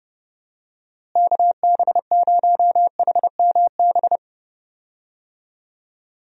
Morse code, heard as a keyed tone, about 30 wpm, 700 Hz. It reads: KB0HMB
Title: USA Call Signs - 6 Characters - Repeated 2x - 30wpm